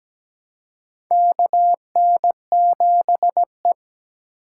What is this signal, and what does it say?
Morse code, heard as a keyed tone, about 17 wpm, 700 Hz. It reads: KN7E
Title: USA Call Signs - 4 Characters - 2x Character Spacing - 17wpm